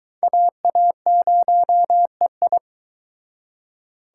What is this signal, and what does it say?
Morse code, heard as a keyed tone, about 23 wpm, 700 Hz. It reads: AA0EI